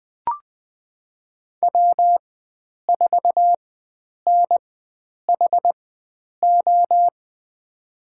Morse code, heard as a keyed tone, about 20 wpm, spaced as for 9 wpm, 700 Hz. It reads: W4NHO
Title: USA Call Signs - 5 Characters - 4x Character Spacing - 20wpm